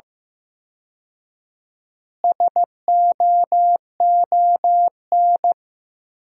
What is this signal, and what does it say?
Morse code, heard as a keyed tone, about 15 wpm, 700 Hz. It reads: SOON
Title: 4x Word-Spacing - Sentences from Top 400 Words - 15wpm